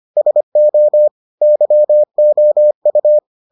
Morse code, heard as a keyed tone, about 25 wpm, 600 Hz. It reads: SO YOU